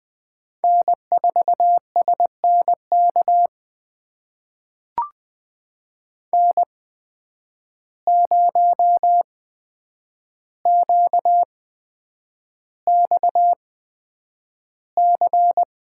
Morse code, heard as a keyed tone, about 20 wpm, spaced as for 9 wpm, 700 Hz. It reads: N4SNK N0QXC
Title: USA Call Signs - 5 Characters - 8x Character Spacing - 20wpm